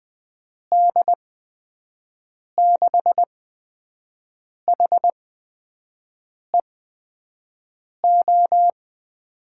Morse code, heard as a keyed tone, about 20 wpm, spaced as for 5 wpm, 700 Hz. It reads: D6HEO
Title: USA Call Signs - 6 Characters - 8x Character Spacing - 20wpm